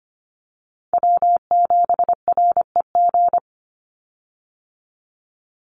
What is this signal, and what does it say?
Morse code, heard as a keyed tone, about 25 wpm, 700 Hz. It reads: W7REG
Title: USA Call Signs 25wpm